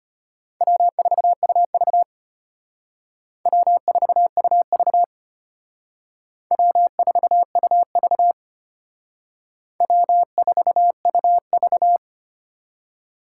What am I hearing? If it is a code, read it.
Morse code, heard as a keyed tone, about 32 wpm, 700 Hz. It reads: W4UV W4UV W4UV W4UV